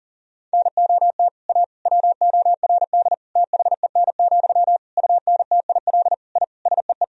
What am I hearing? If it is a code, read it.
Morse code, heard as a keyed tone, about 40 wpm, 700 Hz. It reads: NOT A WORD THEN, UNTIL I SEE
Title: Sentences from Top 1000 Words - Review Part 1 - 40wpm